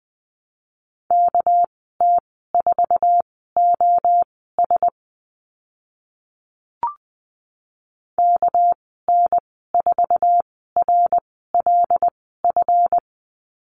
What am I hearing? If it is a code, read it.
Morse code, heard as a keyed tone, about 20 wpm, spaced as for 14 wpm, 700 Hz. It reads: KT4OS KN4RLF
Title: 231 Lesson 37g - Letter x - Call Signs 20wpm